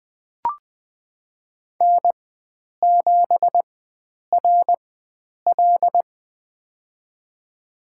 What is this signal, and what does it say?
Morse code, heard as a keyed tone, about 20 wpm, spaced as for 9 wpm, 700 Hz. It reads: N7RL
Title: USA Call Signs - 4 Characters - 4x Character Spacing - 20wpm